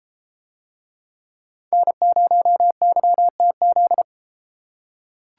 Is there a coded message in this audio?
Yes: Morse code from 1.7 to 4.0 s, heard as a keyed tone, about 34 wpm, 700 Hz. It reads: N0YTZ